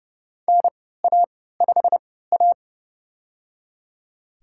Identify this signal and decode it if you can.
Morse code, heard as a keyed tone, about 30 wpm, spaced as for 17 wpm, 700 Hz. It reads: NA5A
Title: USA Call Signs - 4 Characters - 3x Character Spacing - 30wpm